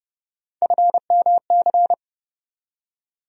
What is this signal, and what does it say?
Morse code, heard as a keyed tone, about 30 wpm, 700 Hz. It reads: FMC
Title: USA Call Sign Postfixes - 3 Letters 30wpm